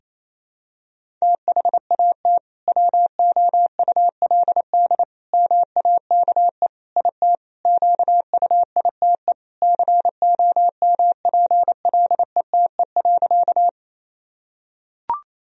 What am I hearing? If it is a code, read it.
Morse code, heard as a keyed tone, about 28 wpm, 700 Hz. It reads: THAT WOULD MAKE IT QUITE COMPLETE.
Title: Sentences from Top 1000 Words - Review Part 2 - 28wpm